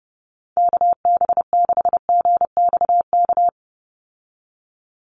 Morse code, heard as a keyed tone, about 30 wpm, 700 Hz. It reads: KB6GXK
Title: USA Call Signs 30wpm